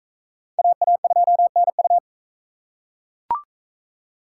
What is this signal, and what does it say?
Morse code, heard as a keyed tone, about 41 wpm, 700 Hz. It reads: AA2NU